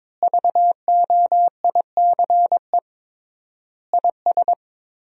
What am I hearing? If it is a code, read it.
Morse code, heard as a keyed tone, about 21 wpm, 700 Hz. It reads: VOICE IS
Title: 3x Word-Spacing - Sentences from Top 500 Words - 22wpm